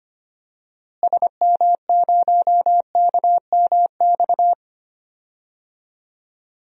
Morse code, heard as a keyed tone, about 25 wpm, 700 Hz. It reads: SM0KMX
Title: DXCC-100 Call Signs - 6 Characters - Repeated 2x - 25wpm